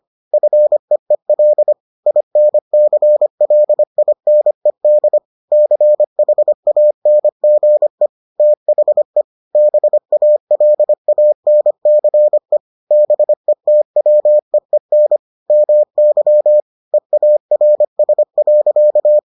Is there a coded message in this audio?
Yes: Morse code from 0.3 to 19.2 s, heard as a keyed tone, about 25 wpm, 600 Hz. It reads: FEEL INCLINED CHANGE THE BALANCE BETWEEN MY EARS.